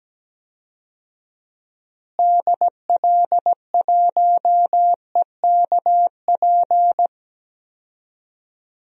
Morse code, heard as a keyed tone, about 17 wpm, 700 Hz. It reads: DL1EKP
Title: DXCC-100 Call Signs - 6 Characters - Repeated 2x - 17wpm